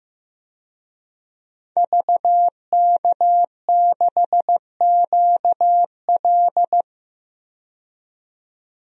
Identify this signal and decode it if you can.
Morse code, heard as a keyed tone, about 15 wpm, 700 Hz. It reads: VK6QL
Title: DXCC-100 Call Signs - 5 Characters - 15wpm